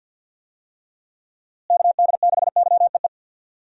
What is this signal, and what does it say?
Morse code, heard as a keyed tone, about 51 wpm, 700 Hz. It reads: KD6YEE
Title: USA Call Signs - 6 Characters - Repeated 3x - 50wpm